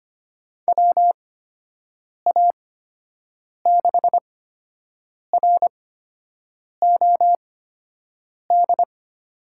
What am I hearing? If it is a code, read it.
Morse code, heard as a keyed tone, about 25 wpm, spaced as for 7 wpm, 700 Hz. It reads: WA6ROD
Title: USA Call Signs - 6 Characters - 8x Character Spacing - 25wpm